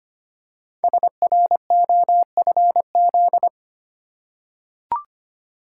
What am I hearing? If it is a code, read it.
Morse code, heard as a keyed tone, about 25 wpm, 700 Hz. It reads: SROFZ
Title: Code Groups - 5 Letters - 25wpm